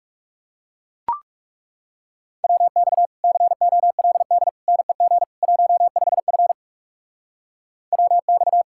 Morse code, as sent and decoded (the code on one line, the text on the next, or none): .-- -..- / -.-. --- .-.. -.. / -. --. / .---- ..... ..-. / .-- -..-
WX COLD NEG 15F WX